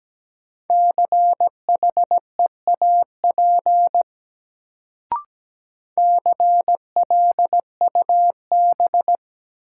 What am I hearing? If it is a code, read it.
Morse code, heard as a keyed tone, about 17 wpm, 700 Hz. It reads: CHEAP CLUB
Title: Sentences from Top 1200 Words - New Word Drill - 17wpm